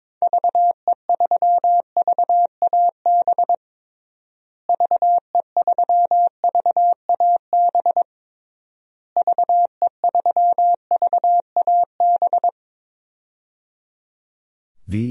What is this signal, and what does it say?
Morse code, heard as a keyed tone, about 22 wpm, 700 Hz. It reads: VE3VAB VE3VAB VE3VAB
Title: DXCC-100 Call Signs - 6 Characters - Repeated 3x - 22wpm